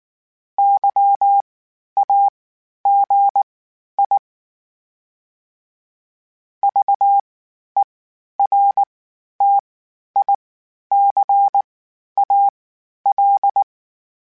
Morse code, heard as a keyed tone, about 19 wpm, spaced as for 11 wpm, 800 Hz. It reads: YAGI VERTICAL
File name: (spaced) / FromSMs spaced mp3s / Antennas sp@18